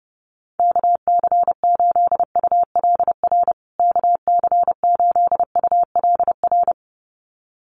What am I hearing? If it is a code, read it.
Morse code, heard as a keyed tone, about 28 wpm, 700 Hz. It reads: KC8ULR KC8ULR
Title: USA Call Signs Repeated 2x 30wpm